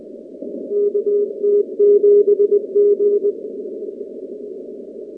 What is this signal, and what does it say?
Morse code, heard as a keyed tone, about 20 wpm, 410 Hz. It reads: KT7G
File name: contests / 2019-cw-wpx-kt5j-k5pi- -2330 0-38